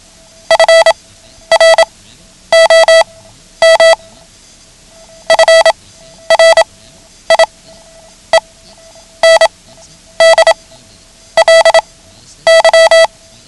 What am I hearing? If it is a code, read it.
Morse code, heard as a keyed tone, about 27 wpm, spaced as for 10 wpm, 660 Hz. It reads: FROM FRIENDLY
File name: ham radio / CodeQuick / Code Quick 09